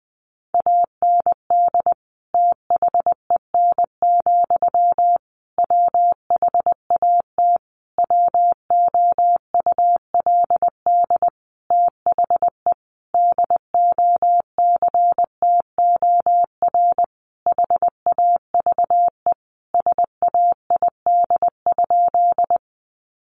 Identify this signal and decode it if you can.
Morse code, heard as a keyed tone, about 20 wpm, 700 Hz. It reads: AND THEN, WHAT WOULD THE DOCTOR HAVE SAID?
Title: Sentences from Top 1500 Words 20wpm